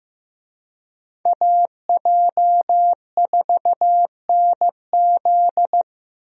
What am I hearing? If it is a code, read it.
Morse code, heard as a keyed tone, about 16 wpm, 700 Hz. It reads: AJ4NZ
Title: USA Call Signs - 5 Characters - Repeated 2x - 15wpm